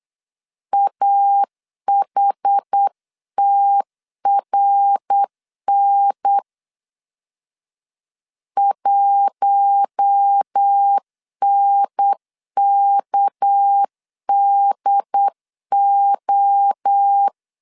Morse code, heard as a keyed tone, about 8 wpm, 800 Hz. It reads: AHTRN 1NKDO
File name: CW / DET / 1 6 1500C S B 8wpm 5142